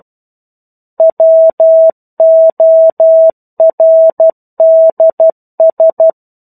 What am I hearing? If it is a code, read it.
Morse code, heard as a keyed tone, about 12 wpm, 650 Hz. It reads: WORDS
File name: 01-Chapter-01 12 WPM